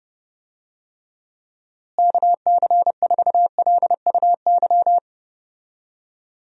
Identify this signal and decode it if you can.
Morse code, heard as a keyed tone, about 30 wpm, 700 Hz. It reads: KC4LUY
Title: USA Call Signs - 6 Characters - Repeated 3x - 30wpm